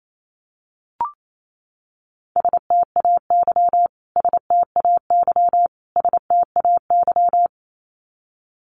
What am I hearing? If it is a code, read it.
Morse code, heard as a keyed tone, about 28 wpm, 700 Hz. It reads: STAY STAY STAY
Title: Top 500 Words Repeated 28wpm